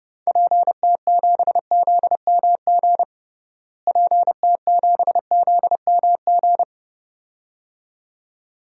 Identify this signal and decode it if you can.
Morse code, heard as a keyed tone, about 30 wpm, 700 Hz. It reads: PT7ZMG PT7ZMG
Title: DXCC-100 Call Signs - 6 Characters - Repeated 2x - 30wpm